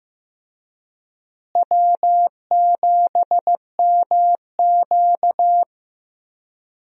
Morse code, heard as a keyed tone, about 15 wpm, 700 Hz. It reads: W7MQ
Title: USA Call Signs - 4 Characters - Repeated 3x - 15wpm